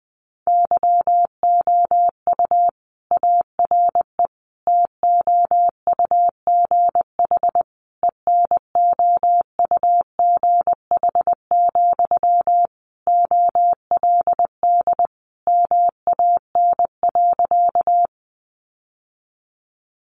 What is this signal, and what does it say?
Morse code, heard as a keyed tone, about 20 wpm, 700 Hz. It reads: YOU ARE TOUGH ENOUGH, OLD MAN.
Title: Sentences from Top 1200 Words 20wpm